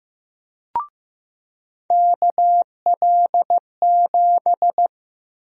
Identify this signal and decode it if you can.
Morse code, heard as a keyed tone, about 15 wpm, 700 Hz. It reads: KL7
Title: USA Call Sign Prefixes - 15wpm